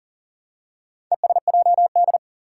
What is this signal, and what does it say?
Morse code, heard as a keyed tone, about 43 wpm, 700 Hz. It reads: ESJD
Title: Four Challenging Letter-Number Combinations 40wpm